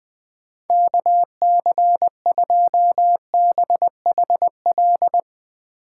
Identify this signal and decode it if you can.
Morse code, heard as a keyed tone, about 20 wpm, 700 Hz. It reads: KC2BHL